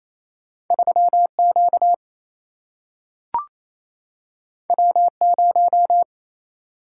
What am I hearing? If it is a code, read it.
Morse code, heard as a keyed tone, about 28 wpm, 700 Hz. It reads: 3Q W0
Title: Single-Letter Single-Number 28wpm